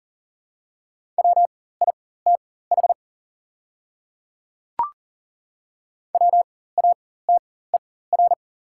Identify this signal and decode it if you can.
Morse code, heard as a keyed tone, about 40 wpm, spaced as for 19 wpm, 690 Hz. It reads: WITH WATER
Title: 4x Character Spacing - Top 200 Words - 40wpm